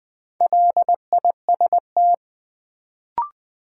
Morse code, heard as a keyed tone, about 20 wpm, 700 Hz. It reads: LIST
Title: Sentences from Top 400 Words - New Word Drill - 20wpm